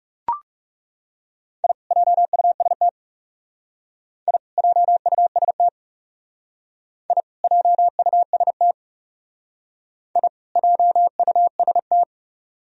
Morse code, heard as a keyed tone, about 36 wpm, 700 Hz. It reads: I JUST I JUST I JUST I JUST
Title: Speed-Racing - Sets of 2 Words 30wpm